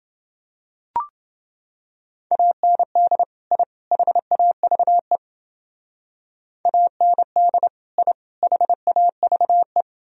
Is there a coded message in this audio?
Yes: Morse code from 2.3 to 9.8 s, heard as a keyed tone, about 30 wpm, 700 Hz. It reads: AND I HAVE AND I HAVE